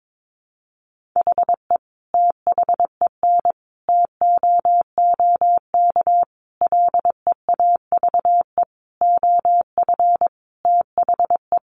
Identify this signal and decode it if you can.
Morse code, heard as a keyed tone, about 22 wpm, 700 Hz. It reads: HE THEN TOOK LEAVE OF THE
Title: Sentences from Top 1400 Words 22wpm